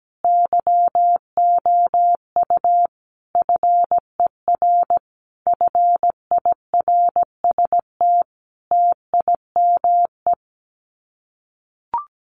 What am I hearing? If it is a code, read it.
Morse code, heard as a keyed tone, about 17 wpm, 700 Hz. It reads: YOU FER FIRST TIME
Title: Common QSO Phrases 17wpm